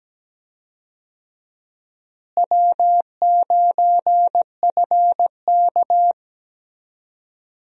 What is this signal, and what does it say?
Morse code, heard as a keyed tone, about 17 wpm, 700 Hz. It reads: W9FK
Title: USA Call Signs - 4 Characters - Repeated 2x - 17wpm